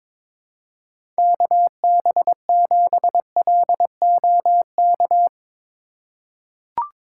Morse code, heard as a keyed tone, about 22 wpm, 700 Hz. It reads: KB7LOK
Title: USA Call Signs - 6 Characters - 4x Character Spacing - 22wpm